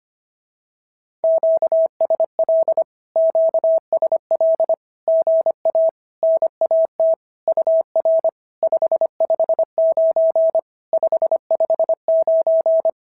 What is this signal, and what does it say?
Morse code, heard as a keyed tone, about 25 wpm, 650 Hz. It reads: QSL QSL GA NAT UR 559 559
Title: SOTA QSO Practice 25wpm